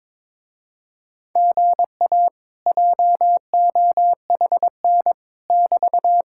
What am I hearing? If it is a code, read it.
Morse code, heard as a keyed tone, about 22 wpm, 700 Hz. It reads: GA JOHN <BT>
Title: Speed-Racing Common QSO Phrases Encore - 22wpm